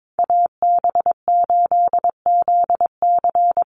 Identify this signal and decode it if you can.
Morse code, heard as a keyed tone, about 22 wpm, 700 Hz. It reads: AB8ZC
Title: USA Call Signs 22wpm